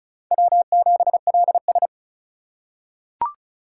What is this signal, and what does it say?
Morse code, heard as a keyed tone, about 36 wpm, 700 Hz. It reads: W7LS
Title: USA Call Signs - 4 Characters - 8x Character Spacing - 35wpm